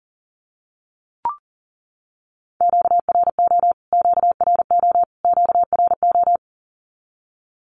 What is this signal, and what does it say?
Morse code, heard as a keyed tone, about 40 wpm, 700 Hz. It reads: QRO QRO QRO
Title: Q-Codes Repeated 3x 40wpm